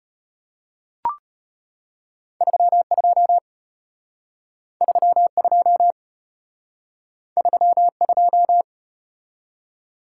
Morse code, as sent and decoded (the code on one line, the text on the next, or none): ...-- ..--- / ...-- ..--- / ...-- ..---
32 32 32